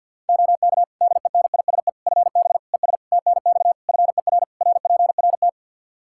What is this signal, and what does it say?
Morse code, heard as a keyed tone, about 52 wpm, 700 Hz. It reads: OK DENISE FB ES TNX FER RPRT
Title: K1USN SST QSO Practice 50wpm